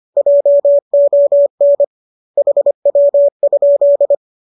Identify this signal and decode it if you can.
Morse code, heard as a keyed tone, about 25 wpm, 570 Hz. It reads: JON HW?